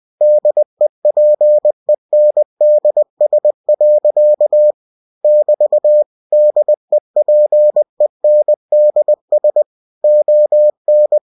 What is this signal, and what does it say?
Morse code, heard as a keyed tone, about 20 wpm, 600 Hz. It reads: DEPENDS. <BT> DEPENDS ON
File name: cw-books / winnie-the-pooh / winnie-20wpm-page-0099